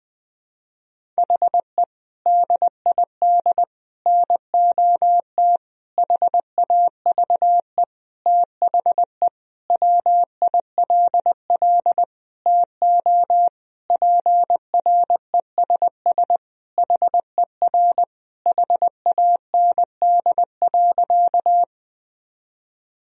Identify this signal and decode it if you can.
Morse code, heard as a keyed tone, about 20 wpm, 700 Hz. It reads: HE DID NOT HAVE THE WILL TO PRESS HER HAND.
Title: Sentences from Top 1000 Words - Review Part 1 - 20wpm